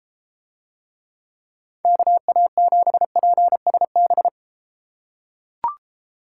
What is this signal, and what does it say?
Morse code, heard as a keyed tone, about 32 wpm, 700 Hz. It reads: KA7PSB